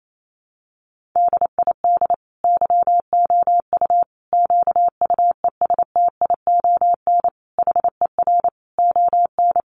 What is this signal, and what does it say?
Morse code, heard as a keyed tone, about 28 wpm, 700 Hz. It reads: DID YOU QUESTION HER ON
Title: Sentences from Top 800 Words 28wpm